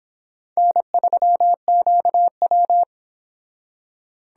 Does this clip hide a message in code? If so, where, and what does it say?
Yes: Morse code from 0.6 to 2.8 s, heard as a keyed tone, about 24 wpm, 700 Hz. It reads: N3QW